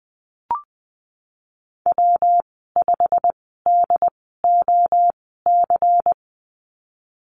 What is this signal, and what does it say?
Morse code, heard as a keyed tone, about 20 wpm, spaced as for 15 wpm, 700 Hz. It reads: W5DOC
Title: 161 Lesson 26g - Letter g - Call Signs 20wpm